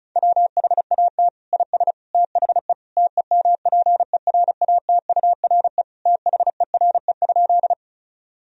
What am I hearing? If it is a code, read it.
Morse code, heard as a keyed tone, about 36 wpm, 700 Hz. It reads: WHAT IS THE TEMPERATURE THERE?
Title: Sentences from Top 1000 Words - Review Part 4 - 35wpm